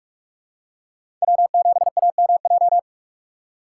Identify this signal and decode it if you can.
Morse code, heard as a keyed tone, about 41 wpm, 700 Hz. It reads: W7AMJ
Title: USA Call Signs - 5 Characters - 6x Character Spacing - 45wpm